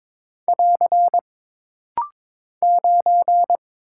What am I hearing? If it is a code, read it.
Morse code, heard as a keyed tone, about 21 wpm, 700 Hz. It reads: <AR> 9
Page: This rapid-fire timing is geared to push your ICR (Instant Character Recognition) skills.